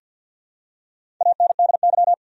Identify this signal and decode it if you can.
Morse code, heard as a keyed tone, about 51 wpm, 700 Hz. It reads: ANDY
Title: USA Common Names - Max 4 Letters - 50wpm